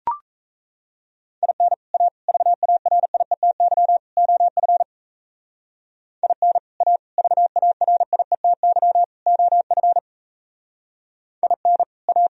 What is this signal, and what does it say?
Morse code, heard as a keyed tone, about 35 wpm, 700 Hz. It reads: IN A VARIETY OF IN A VARIETY OF IN A
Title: Speed-Racing - Sets of 4 Words 28wpm